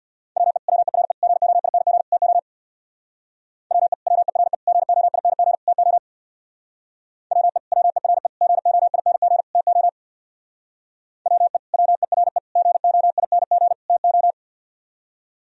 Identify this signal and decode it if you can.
Morse code, heard as a keyed tone, about 62 wpm, 700 Hz. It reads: WE WERE GOING TO WE WERE GOING TO WE WERE GOING TO WE WERE GOING TO
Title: Speed-Racing - Sets of 4 Words 50wpm